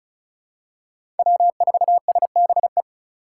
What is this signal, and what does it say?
Morse code, heard as a keyed tone, about 36 wpm, 700 Hz. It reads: W4SBE
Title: USA Call Signs - 5 Characters - 3x Character Spacing - 35wpm